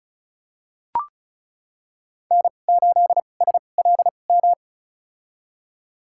Morse code, heard as a keyed tone, about 34 wpm, spaced as for 25 wpm, 700 Hz. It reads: N8SLM
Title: USA Call Signs - 5 Characters - 2x Character Spacing - 35wpm